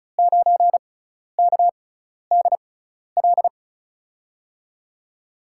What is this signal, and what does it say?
Morse code, heard as a keyed tone, about 36 wpm, spaced as for 12 wpm, 700 Hz. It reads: 9KDL
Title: USA Call Signs - 5 Characters - 6x Character Spacing - 35wpm